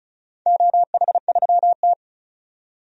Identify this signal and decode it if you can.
Morse code, heard as a keyed tone, about 36 wpm, 700 Hz. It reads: OH3T
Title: DXCC-100 Call Signs - 4 Characters - 35wpm